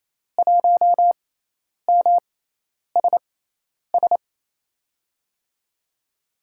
Morse code, heard as a keyed tone, about 28 wpm, spaced as for 10 wpm, 700 Hz. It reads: 1MSS
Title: USA Call Signs - 6 Characters - 6x Character Spacing - 28wpm